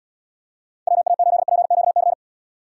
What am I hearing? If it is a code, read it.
Morse code, heard as a keyed tone, about 77 wpm, 700 Hz. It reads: WA8PQC